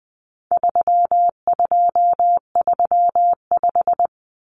Morse code, heard as a keyed tone, about 20 wpm, 700 Hz. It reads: 3235
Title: Numbers - 4-Digit 20wpm